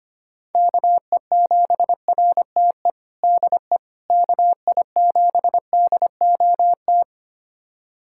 Morse code, heard as a keyed tone, about 25 wpm, 700 Hz. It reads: KE7RTE DE KI7DOT
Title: USA Callsign DE Callsign - 6 Characters - 25wpm